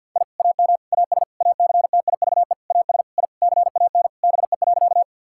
Morse code, heard as a keyed tone, about 47 wpm, 700 Hz. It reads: I AM AS ACTIVE AS I CAN BE.
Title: Sentences from Top 1000 Words - Review Part 6 - 50wpm